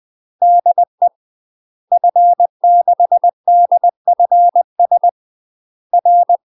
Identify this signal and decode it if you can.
Morse code, heard as a keyed tone, about 20 wpm, 700 Hz. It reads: DE F6DFS R
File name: UK MASTER QSOs TRAINING/20WPM